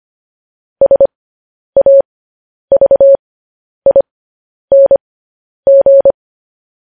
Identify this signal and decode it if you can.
Morse code, heard as a keyed tone, about 25 wpm, spaced as for 10 wpm, 550 Hz. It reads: SAVING
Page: news-headlines-in-morse-code-at-10-wpm